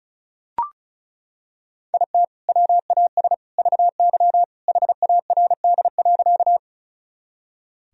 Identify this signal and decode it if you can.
Morse code, heard as a keyed tone, about 34 wpm, 700 Hz. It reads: IT WAS VY HARD.
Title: Sentences from Top 300 Words and Abbreviations 35wpm